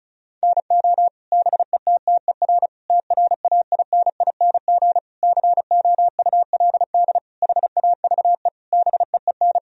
Morse code, heard as a keyed tone, about 36 wpm, 700 Hz. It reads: NO BETTER TRAINING COULD HAVE BEEN
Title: Sentences from Top 1000 Words - Review Part 4 - 35wpm